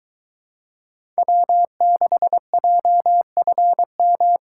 Morse code, heard as a keyed tone, about 22 wpm, 700 Hz. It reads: W6JFM